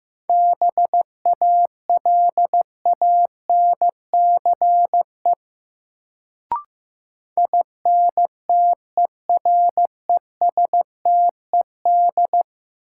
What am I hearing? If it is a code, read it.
Morse code, heard as a keyed tone, about 15 wpm, 700 Hz. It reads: BALANCE INTERESTED
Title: Sentences from Top 900 Words - New Word Drill - 15wpm